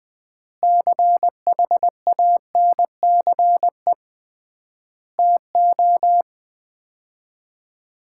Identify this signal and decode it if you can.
Morse code, heard as a keyed tone, about 20 wpm, 700 Hz. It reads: CHANCE TO
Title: 3x Word-Spacing - Sets of 4 Words - 20wpm